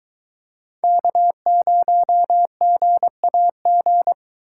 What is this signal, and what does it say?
Morse code, heard as a keyed tone, about 22 wpm, 700 Hz. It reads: K0GAG